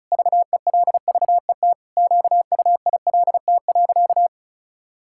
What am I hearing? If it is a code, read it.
Morse code, heard as a keyed tone, about 36 wpm, 700 Hz. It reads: VELVET QUILT.
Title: Pangram Challenge v2 - 35wpm